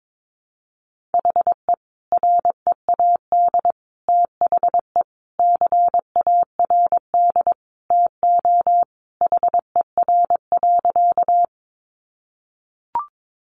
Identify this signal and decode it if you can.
Morse code, heard as a keyed tone, about 22 wpm, 700 Hz. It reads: HE READ THE CARD TO HER.
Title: Sentences from Top 600 Words 22wpm